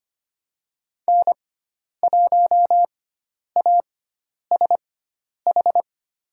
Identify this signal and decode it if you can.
Morse code, heard as a keyed tone, about 25 wpm, spaced as for 10 wpm, 700 Hz. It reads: N1ASH
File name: Lesson 16g - Letter c - Call Signs 25wpm